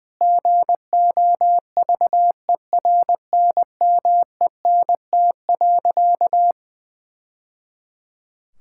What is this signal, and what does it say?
Morse code, heard as a keyed tone, about 21 wpm, 700 Hz. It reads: GOVERNMENT.
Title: Sentences from Top 1000 Words - Review Part 6 - 20wpm